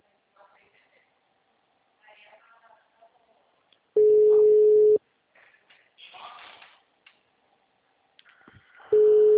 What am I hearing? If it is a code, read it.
Ringback tone (Europe, 425 Hz), with speech.